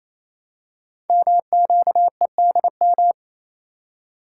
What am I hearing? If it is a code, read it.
Morse code, heard as a keyed tone, about 29 wpm, 700 Hz. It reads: MQEDM
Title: Code Groups - 5 Letter:Numbers - 28wpm